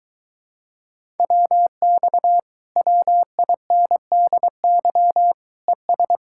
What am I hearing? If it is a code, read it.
Morse code, heard as a keyed tone, about 22 wpm, 700 Hz. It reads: WX WINDY ES